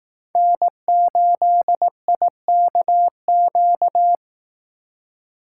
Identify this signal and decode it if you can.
Morse code, heard as a keyed tone, about 18 wpm, 700 Hz. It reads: N8IKQ
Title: Speed Racing - USA 1x3 Call Signs 15wpm